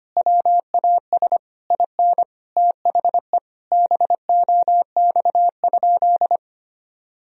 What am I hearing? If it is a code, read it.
Morse code, heard as a keyed tone, about 25 wpm, 700 Hz. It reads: WAS IN THE BOX?
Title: Sentences from Top 1000 Words - Review Part 2 - 25wpm